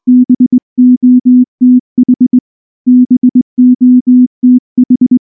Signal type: Morse code